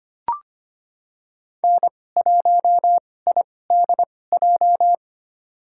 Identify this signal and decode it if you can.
Morse code, heard as a keyed tone, about 25 wpm, spaced as for 18 wpm, 700 Hz. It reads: N1IDJ
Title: USA Call Signs - 5 Characters - 2x Character Spacing - 25wpm